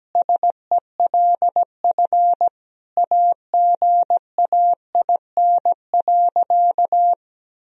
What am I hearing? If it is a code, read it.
Morse code, heard as a keyed tone, about 17 wpm, 700 Hz. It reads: SELF AGAIN.
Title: Sentences from Top 1000 Words - Review Part 1 - 17wpm